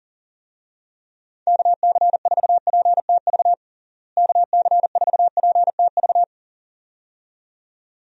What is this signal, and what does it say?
Morse code, heard as a keyed tone, about 40 wpm, 700 Hz. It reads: KC4PTV KC4PTV
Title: USA Call Signs - 6 Characters - Repeated 2x - 40wpm